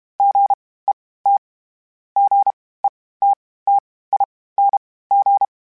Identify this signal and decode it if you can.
Morse code, heard as a keyed tone, about 32 wpm, spaced as for 18 wpm, 800 Hz. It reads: GET GETTING